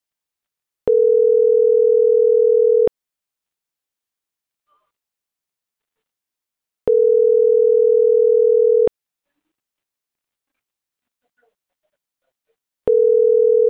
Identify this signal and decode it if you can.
Ringback tone (North America), 440 + 480 Hz.